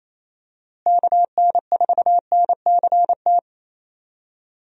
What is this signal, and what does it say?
Morse code, heard as a keyed tone, about 28 wpm, 700 Hz. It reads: KN4NCT
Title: USA Call Signs - 6 Characters - Repeated 2x - 28wpm